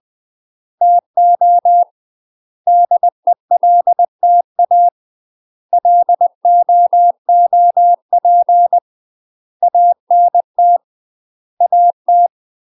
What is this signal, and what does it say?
Morse code, heard as a keyed tone, about 20 wpm, 700 Hz. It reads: TO DELTA LOOP ANT AT
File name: UK MASTER QSOs TRAINING / 20WPM / UK MASTER QSOs 0029